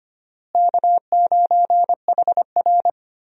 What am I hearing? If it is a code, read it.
Morse code, heard as a keyed tone, about 25 wpm, 700 Hz. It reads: K9HR
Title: USA Call Signs - 4 Characters - Repeated 3x - 25wpm